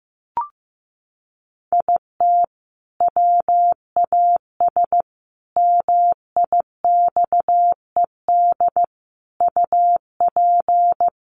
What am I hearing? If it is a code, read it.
Morse code, heard as a keyed tone, about 15 wpm, 700 Hz. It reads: IT WAS MIXED UP
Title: Sentences from Top 1500 Words 15wpm